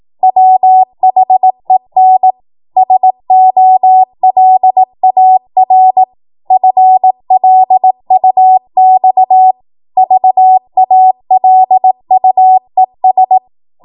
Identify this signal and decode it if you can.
Morse code, heard as a keyed tone, about 18 wpm, 750 Hz. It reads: WHEN SOLAR FLUX VALUES
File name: Morse / 18WPM